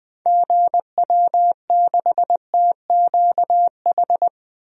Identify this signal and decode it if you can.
Morse code, heard as a keyed tone, about 21 wpm, 700 Hz. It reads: GW6TQH
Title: DXCC-100 Call Signs - 6 Characters - Repeated 2x - 20wpm